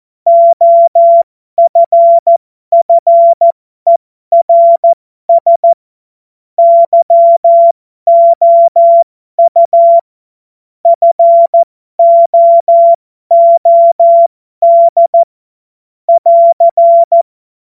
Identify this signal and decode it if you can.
Morse code, heard as a keyed tone, about 14 wpm, spaced as for 12 wpm, 680 Hz. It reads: OFFERS YOU FOOD <AR>
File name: Morsekurs / Witze